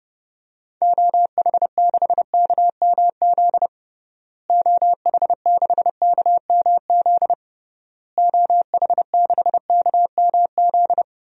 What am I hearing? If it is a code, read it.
Morse code, heard as a keyed tone, about 32 wpm, 700 Hz. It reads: OH6KMZ OH6KMZ OH6KMZ
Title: DXCC-100 Call Signs - 6 Characters - Repeated 3x - 30wpm